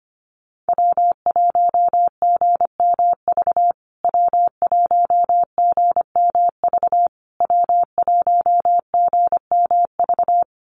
Morse code, heard as a keyed tone, about 25 wpm, 700 Hz. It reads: W1GMV W1GMV W1GMV
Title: USA Call Signs Repeated 3x 25wpm